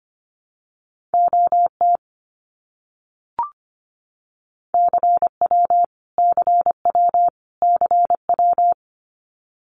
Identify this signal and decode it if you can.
Morse code, heard as a keyed tone, about 25 wpm, 700 Hz. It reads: OT CW CW CW
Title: Abbreviations Repeated 3x - 25wpm